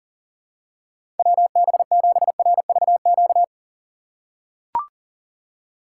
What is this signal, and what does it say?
Morse code, heard as a keyed tone, about 40 wpm, 700 Hz. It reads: WB7RVQ
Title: USA Call Signs - 6 Characters - 4x Character Spacing - 40wpm